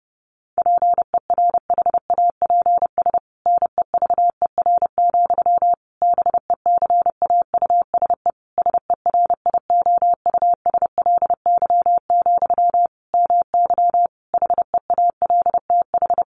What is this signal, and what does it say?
Morse code, heard as a keyed tone, about 30 wpm, 700 Hz. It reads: PERHAPS NEVER, BECAUSE SERIOUSLY, MY HEALTH
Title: Sentences from Top 1200 Words 30wpm